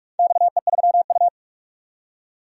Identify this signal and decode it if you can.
Morse code, heard as a keyed tone, about 46 wpm, 700 Hz. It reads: XE3U
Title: DXCC-100 Call Signs - 4 Characters - 45wpm